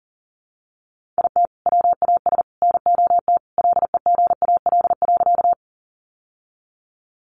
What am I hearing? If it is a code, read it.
Morse code, heard as a keyed tone, about 40 wpm, 700 Hz. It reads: IT WAS NOT LEGAL.
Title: Sentences from Top 1200 Words 40wpm